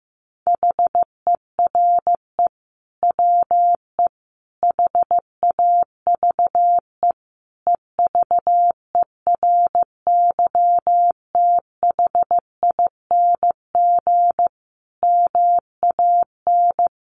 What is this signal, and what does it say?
Morse code, heard as a keyed tone, about 15 wpm, 700 Hz. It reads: HERE WE HAVE EVERYTHING MAN
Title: Sentences from Top 1100 Words 15wpm